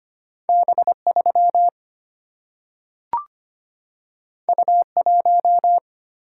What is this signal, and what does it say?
Morse code, heard as a keyed tone, about 25 wpm, 700 Hz. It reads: B3 U1
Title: Single-Letter Single-Number 25wpm